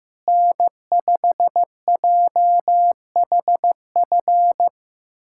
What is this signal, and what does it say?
Morse code, heard as a keyed tone, about 15 wpm, 700 Hz. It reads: N5JHF
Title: USA Callsign DE Callsign - 5 Characters - 15wpm